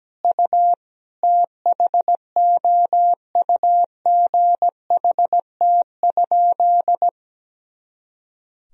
Morse code, heard as a keyed tone, about 17 wpm, 700 Hz. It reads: U THOUGHT?
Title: Sentences from Top 300 Words and Abbreviations 17wpm